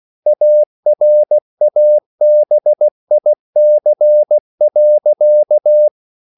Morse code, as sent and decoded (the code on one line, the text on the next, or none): .- .-. .- -... .. -.-. .-.-.-
ARABIC.